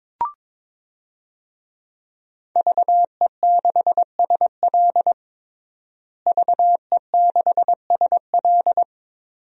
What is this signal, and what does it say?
Morse code, heard as a keyed tone, about 22 wpm, 700 Hz. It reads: VE6SL VE6SL
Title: DXCC-100 Call Signs - 5 Characters - Repeated 2x - 22wpm